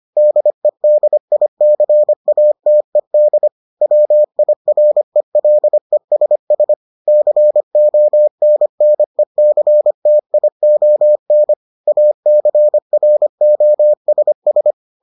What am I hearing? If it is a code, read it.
Morse code, heard as a keyed tone, about 25 wpm, 600 Hz. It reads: DEDICATED WIRELESS CONNECTION ACROSS